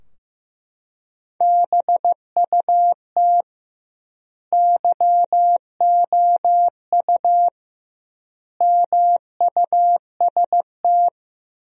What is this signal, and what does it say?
Morse code, heard as a keyed tone, about 16 wpm, 700 Hz. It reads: BUT YOU MUST
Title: 2x Word-Spacing - Sentences from Top 300 Words - 15wpm